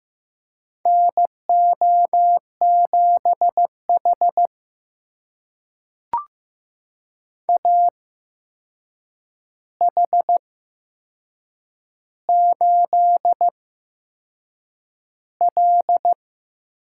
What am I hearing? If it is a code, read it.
Morse code, heard as a keyed tone, about 15 wpm, spaced as for 6 wpm, 700 Hz. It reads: NO7H AH8L
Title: USA Call Signs - 4 Characters - 8x Character Spacing - 15wpm